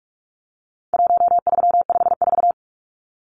Morse code, heard as a keyed tone, about 44 wpm, 700 Hz. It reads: 1354
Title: Numbers - 4-Digit 45wpm